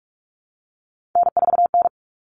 Morse code, heard as a keyed tone, about 44 wpm, 700 Hz. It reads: N4N
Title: All Cut Numbers - 3-Digit 45wpm